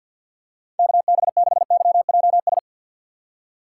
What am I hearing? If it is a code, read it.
Morse code, heard as a keyed tone, about 49 wpm, 690 Hz. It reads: KB6YJS
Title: USA Call Signs - 6 Characters - 3x Character Spacing - 50wpm